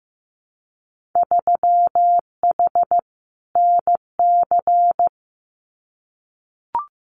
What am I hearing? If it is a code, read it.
Morse code, heard as a keyed tone, about 15 wpm, 700 Hz. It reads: 3H NC
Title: Winter Field Day Exchanges 15wpm